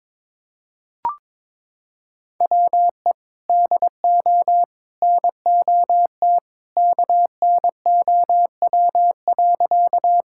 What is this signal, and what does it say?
Morse code, heard as a keyed tone, about 22 wpm, 700 Hz. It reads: WE DO NOT KNOW.
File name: practice / 22wpm / Sentences with 3-Word Phrases - 22wpm